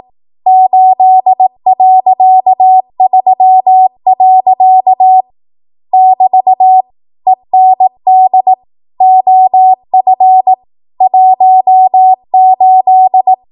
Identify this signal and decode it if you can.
Morse code, heard as a keyed tone, about 18 wpm, 750 Hz. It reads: 8.3. <BT> END OF 18
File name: Archive / 18 WPM / 18WPM